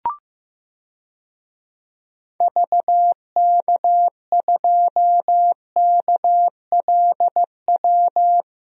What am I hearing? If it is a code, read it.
Morse code, heard as a keyed tone, about 15 wpm, 700 Hz. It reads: VK2KLW
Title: DXCC-100 Call Signs - 6 Characters - 15wpm